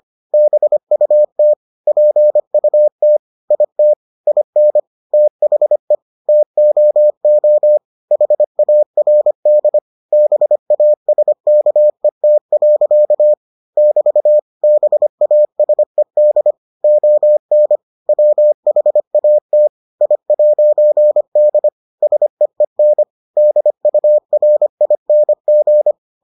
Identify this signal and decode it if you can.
Morse code, heard as a keyed tone, about 25 wpm, 600 Hz. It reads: BUT PUT IT IN THE TOO HARD BASKET. <BT> BASED ON WHAT I'D SEEN DURING